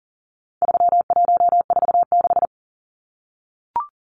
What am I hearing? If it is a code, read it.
Morse code, heard as a keyed tone, about 40 wpm, 700 Hz. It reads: 3146
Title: Numbers - 4-Digit 40wpm